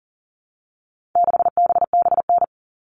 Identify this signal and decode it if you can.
Morse code, heard as a keyed tone, about 40 wpm, 700 Hz. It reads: 6BBN
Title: All Cut Numbers - 4-Digit 40wpm